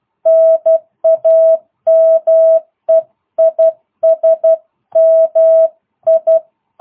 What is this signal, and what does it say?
Morse code, heard as a keyed tone, about 12 wpm, 650 Hz. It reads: NAME IS MI